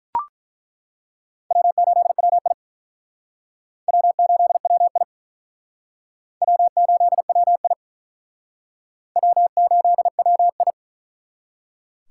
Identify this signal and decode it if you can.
Morse code, heard as a keyed tone, about 48 wpm, 700 Hz. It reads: W8WI W8WI W8WI W8WI